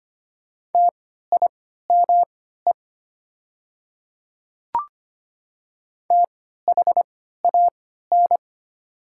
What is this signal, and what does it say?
Morse code, heard as a keyed tone, about 25 wpm, spaced as for 14 wpm, 690 Hz. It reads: TIME THAN
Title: 3x Character Spacing - Top 300 Words - 25wpm